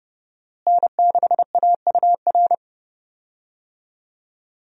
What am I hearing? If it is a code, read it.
Morse code, heard as a keyed tone, about 30 wpm, 700 Hz. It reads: N6AUR
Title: USA Call Signs - 5 Characters - Repeated 3x - 30wpm